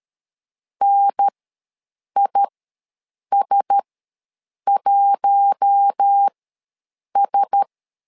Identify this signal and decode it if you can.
Morse code, heard as a keyed tone, about 13 wpm, spaced as for 7 wpm, 800 Hz. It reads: NIS1S
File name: LEZIONI / 4 Q1S ws 12wpm 1827